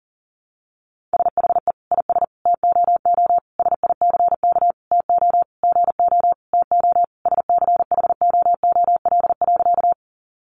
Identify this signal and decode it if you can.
Morse code, heard as a keyed tone, about 37 wpm, 700 Hz. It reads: SHE IS TOO SICK TO GO TO SCHOOL.
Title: Easy Phrases 40wpm